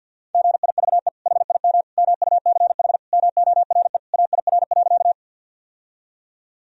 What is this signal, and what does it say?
Morse code, heard as a keyed tone, about 52 wpm, 700 Hz. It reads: GIVE HIM MUCH MORE AIR.
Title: Sentences from Top 1000 Words - Review Part 1 - 50wpm